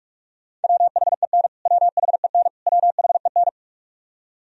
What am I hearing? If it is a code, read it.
Morse code, heard as a keyed tone, about 41 wpm, 700 Hz. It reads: WHEN WHEN WHEN
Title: Top 100 Words Repeated 45wpm